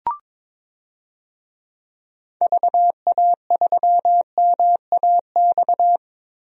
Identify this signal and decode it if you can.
Morse code, heard as a keyed tone, about 22 wpm, 700 Hz. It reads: VA3MAX
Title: DXCC-100 Call Signs - 6 Characters - Repeated 2x - 22wpm